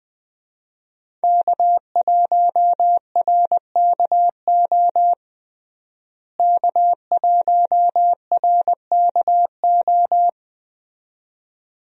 Morse code, heard as a keyed tone, about 20 wpm, 700 Hz. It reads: K1RKO K1RKO
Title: USA Call Signs - 5 Characters - Repeated 2x - 20wpm